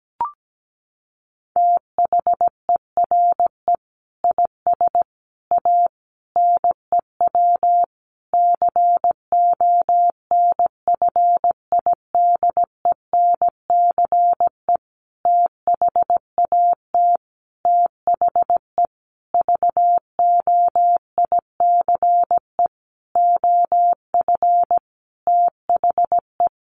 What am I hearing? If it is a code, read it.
Morse code, heard as a keyed tone, about 17 wpm, 700 Hz. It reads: THERE IS A NEW CONFIDENCE THAT THE VOICE OF THE
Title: Sentences from Top 1500 Words 17wpm